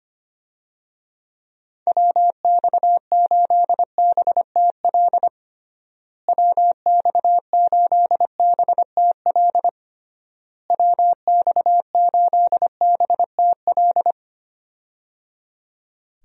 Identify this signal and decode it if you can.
Morse code, heard as a keyed tone, about 25 wpm, 700 Hz. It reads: WX8BTL WX8BTL WX8BTL
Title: USA Call Signs - 6 Characters - Repeated 3x - 25wpm